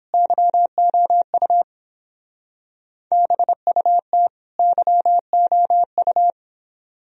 Morse code, heard as a keyed tone, about 29 wpm, 700 Hz. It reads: YOU BUT YOU